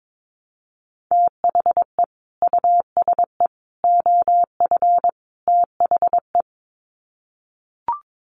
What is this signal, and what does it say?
Morse code, heard as a keyed tone, about 22 wpm, 700 Hz. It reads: THE USE OF THE
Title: Sets of 4 Words 22wpm